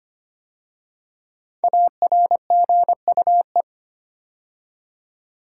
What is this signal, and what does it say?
Morse code, heard as a keyed tone, about 25 wpm, 700 Hz. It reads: ARGUE
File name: 5 Letter Words - First 3 Letters Given - 25wpm